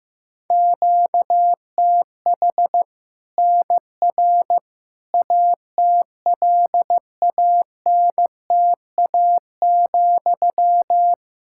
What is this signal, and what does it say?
Morse code, heard as a keyed tone, about 15 wpm, 700 Hz. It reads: QTH NR ATLANTA,
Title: Practice QSO - Exchange 1 - Beginning - 15wpm